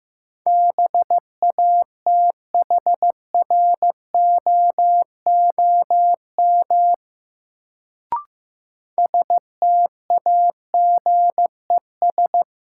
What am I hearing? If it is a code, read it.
Morse code, heard as a keyed tone, about 15 wpm, 700 Hz. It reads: BATHROOM STAGES
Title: Sets of 5 Words Encore - New Word Drill - 15wpm